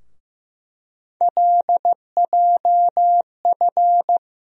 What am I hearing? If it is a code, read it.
Morse code, heard as a keyed tone, about 15 wpm, 700 Hz. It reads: LJF
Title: USA Call Sign Postfixes - 3 Letters 15wpm